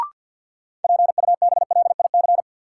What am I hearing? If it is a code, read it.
Morse code, heard as a keyed tone, about 53 wpm, 700 Hz. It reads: PUBLIC
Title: Sentences from Top 600 Words - New Word Drill - 50wpm